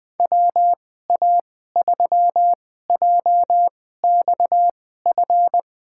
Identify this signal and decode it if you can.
Morse code, heard as a keyed tone, about 20 wpm, spaced as for 14 wpm, 700 Hz. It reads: WA3JXF
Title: USA Call Signs - 6 Characters - 2x Character Spacing - 20wpm